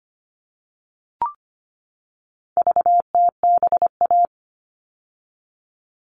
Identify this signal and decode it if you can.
Morse code, heard as a keyed tone, about 25 wpm, 700 Hz. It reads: VTBA
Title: All Cut Numbers - 4-Digit 25wpm